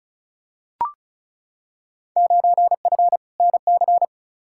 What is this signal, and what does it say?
Morse code, heard as a keyed tone, about 34 wpm, 700 Hz. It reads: 9F NC
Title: Field Day Exchanges Encore - 35wpm